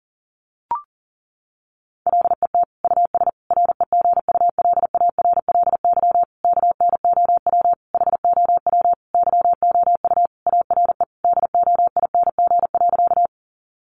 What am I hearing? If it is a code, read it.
Morse code, heard as a keyed tone, about 40 wpm, 700 Hz. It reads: LET US REGULARLY KNOW HOW YOU ARE DOING.
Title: Sentences from Top 1400 Words 40wpm